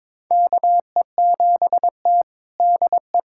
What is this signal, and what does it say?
Morse code, heard as a keyed tone, about 21 wpm, 700 Hz. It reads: KE7T DE
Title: USA Callsign DE Callsign - 4 Characters - 22wpm